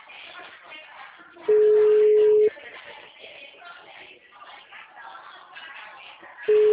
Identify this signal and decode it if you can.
Ringback tone (Europe, 425 Hz), with speech.